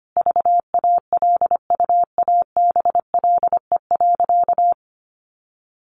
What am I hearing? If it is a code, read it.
Morse code, heard as a keyed tone, about 25 wpm, 700 Hz. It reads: VALUABLE.
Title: Sentences from Top 1200 Words 25wpm